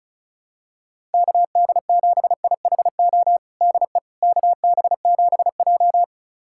Morse code, heard as a keyed tone, about 34 wpm, 700 Hz. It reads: KD7IHO DE KB7J
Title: USA Callsign DE Callsign 35wpm